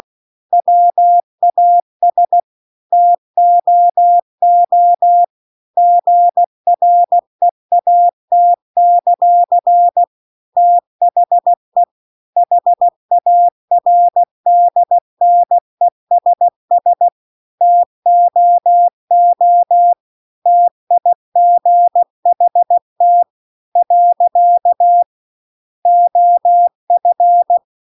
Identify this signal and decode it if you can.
Morse code, heard as a keyed tone, about 16 wpm, 700 Hz. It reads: WAS TOO GREAT; THE HARDNESS TOO TIGHT . OF
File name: CW-The-Big-Twoheaded-River-2-16WPM